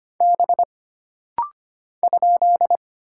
Morse code, heard as a keyed tone, about 25 wpm, 700 Hz. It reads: B ?
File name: Single Character (Letter:Number:Punctuation:Prosign) - Rapid-Fire - 25wpm